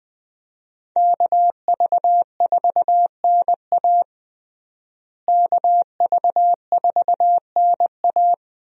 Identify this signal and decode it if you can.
Morse code, heard as a keyed tone, about 20 wpm, 700 Hz. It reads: KV4NA KV4NA
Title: USA Call Signs - 5 Characters - Repeated 2x - 20wpm